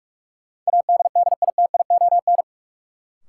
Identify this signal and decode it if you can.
Morse code, heard as a keyed tone, about 46 wpm, 700 Hz. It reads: ADDITION
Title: Sentences from Top 1500 Words - New Word Drill Alt - 45wpm